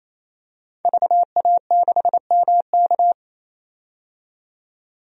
Morse code, heard as a keyed tone, about 28 wpm, 700 Hz. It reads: VA6MK
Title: DXCC-100 Call Signs - 5 Characters - 28wpm